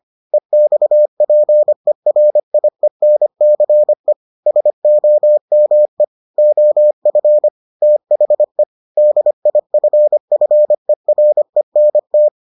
Morse code, heard as a keyed tone, about 25 wpm, 600 Hz. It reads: EXPERIENCE SOME OF THE DIFFERENT